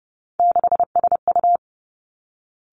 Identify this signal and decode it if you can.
Morse code, heard as a keyed tone, about 30 wpm, 700 Hz. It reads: 6SU
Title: Three Challenging Letter-Number Combinations 30wpm